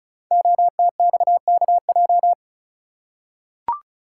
Morse code, heard as a keyed tone, about 36 wpm, 700 Hz. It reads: OTXKJ
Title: Code Groups - 5 Letter:Numbers - 35wpm